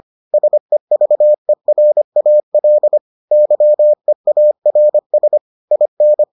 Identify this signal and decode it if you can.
Morse code, heard as a keyed tone, about 25 wpm, 600 Hz. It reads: SEVERAL YEARS IN